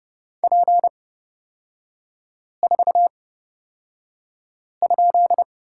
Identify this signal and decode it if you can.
Morse code, heard as a keyed tone, about 30 wpm, spaced as for 5 wpm, 700 Hz. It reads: P4?
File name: Lesson 38e - Letter q - Three Characters 30wpm